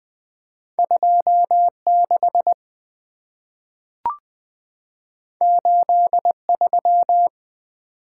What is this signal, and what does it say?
Morse code, heard as a keyed tone, about 20 wpm, 690 Hz. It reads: 26 83
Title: Two Letter-Number Combinations - 20wpm